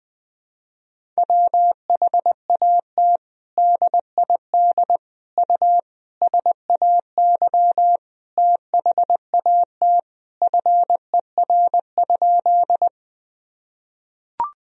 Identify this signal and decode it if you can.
Morse code, heard as a keyed tone, about 20 wpm, 700 Hz. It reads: WHAT DID U SAY THAT FER?
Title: Sentences from Top 200 Words and Abbreviations 20wpm